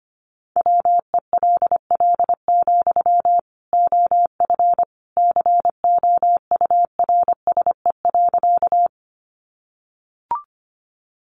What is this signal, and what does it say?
Morse code, heard as a keyed tone, about 25 wpm, 700 Hz. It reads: WELL, OF COURSE.
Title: Sentences from Top 500 Words 25wpm